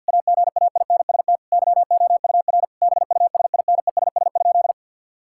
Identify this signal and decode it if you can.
Morse code, heard as a keyed tone, about 52 wpm, 690 Hz. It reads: AGAINST YOUR BUSINESS?
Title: Sentences from Top 1000 Words - Review Part 5 - 50wpm